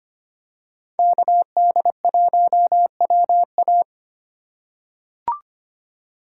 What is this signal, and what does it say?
Morse code, heard as a keyed tone, about 25 wpm, 700 Hz. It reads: KD1WA
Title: USA Call Signs - 5 Characters - 3x Character Spacing - 25wpm